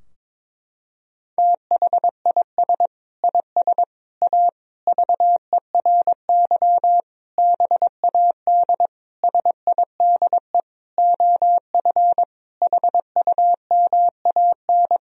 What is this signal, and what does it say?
Morse code, heard as a keyed tone, about 22 wpm, 700 Hz. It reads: THIS IS A VERY BAD SIDE OF HUMAN
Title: Sentences from Top 1000 Words - Review Part 3 - 22wpm